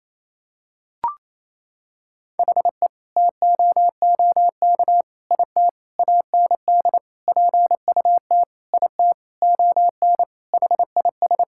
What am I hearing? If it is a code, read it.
Morse code, heard as a keyed tone, about 28 wpm, 700 Hz. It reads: HE TOOK IT AND PUT IT ON HIS
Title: Sentences from Top 1600 Words 28wpm